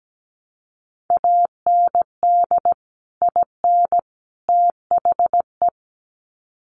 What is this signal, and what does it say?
Morse code, heard as a keyed tone, about 17 wpm, 700 Hz. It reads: AND IN THE
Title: Sets of 3 Words 17wpm